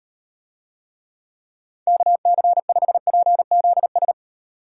Morse code, heard as a keyed tone, about 38 wpm, 700 Hz. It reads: KC5PZS